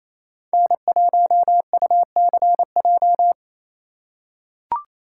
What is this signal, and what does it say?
Morse code, heard as a keyed tone, about 28 wpm, 700 Hz. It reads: N1UCJ
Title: USA Call Signs - 5 Characters - 2x Character Spacing - 28wpm